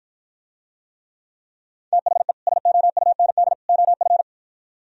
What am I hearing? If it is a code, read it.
Morse code, heard as a keyed tone, about 52 wpm, 700 Hz. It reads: THE SOUND OF